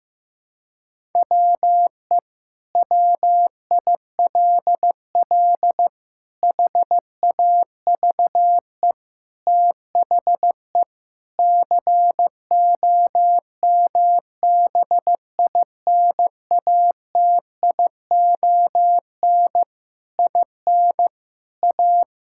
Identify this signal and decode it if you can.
Morse code, heard as a keyed tone, about 16 wpm, 700 Hz. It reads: WE WILL HAVE THE COMBINATION IN A
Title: Sentences from Top 1000 Words - Review Part 6 - 15wpm